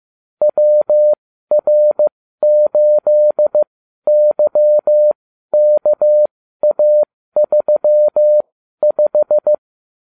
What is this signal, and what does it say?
Morse code, heard as a keyed tone, about 15 wpm, spaced as for 12 wpm, 600 Hz. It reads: WR8YKA35
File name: cw / ubacw34 3